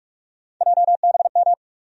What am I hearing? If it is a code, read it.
Morse code, heard as a keyed tone, about 47 wpm, 700 Hz. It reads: JBM
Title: USA Call Sign Postfixes - 3 Letters 45wpm